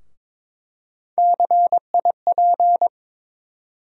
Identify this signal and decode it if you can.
Morse code, heard as a keyed tone, about 22 wpm, 700 Hz. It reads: CIP
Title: USA Call Sign Postfixes - 3 Letters 22wpm